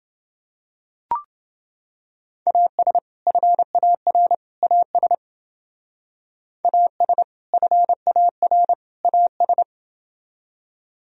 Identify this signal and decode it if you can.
Morse code, heard as a keyed tone, about 29 wpm, 700 Hz. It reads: AS FAR AS AS FAR AS